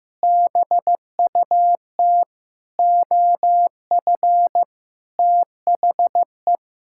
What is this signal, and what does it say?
Morse code, heard as a keyed tone, about 15 wpm, 700 Hz. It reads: BUT OF THE
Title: Sentences from Top 1600 Words 15wpm